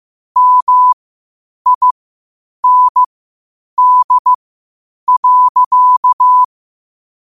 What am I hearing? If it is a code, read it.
Morse code, heard as a keyed tone, about 15 wpm, spaced as for 8 wpm, 1000 Hz. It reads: MIND.
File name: qotdfiles / 8wpm